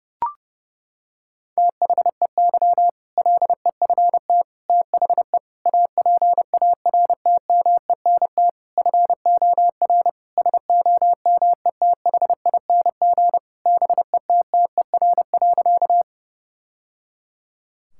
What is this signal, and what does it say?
Morse code, heard as a keyed tone, about 30 wpm, 700 Hz. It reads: THEY LEFT THE APARTMENT FOR SOMETHING BETTER.
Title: Sentences from Top 1000 Words - Review Part 6 - 30wpm